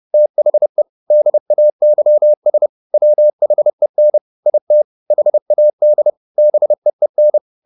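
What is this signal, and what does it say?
Morse code, heard as a keyed tone, about 30 wpm, 600 Hz. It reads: THE DAYS WHEN IT HAD BEEN
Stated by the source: Morse code at 30 wpm, the novel Never by Ken Follett in English, from CW med Gnister.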